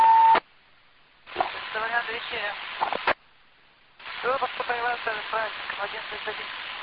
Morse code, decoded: none

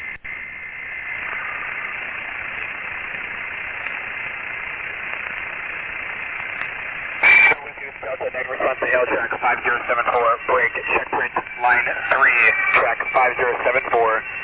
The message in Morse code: none

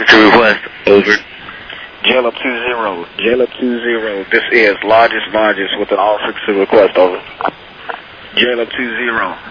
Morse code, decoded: none

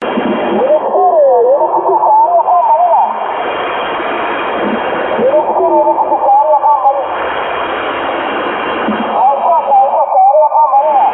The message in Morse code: none

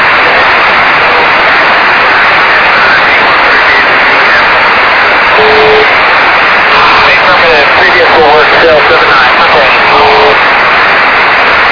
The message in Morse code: none

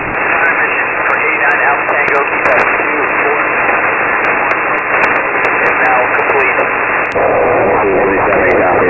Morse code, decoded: none

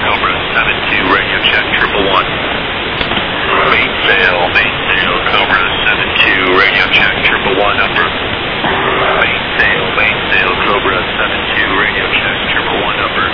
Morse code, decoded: none

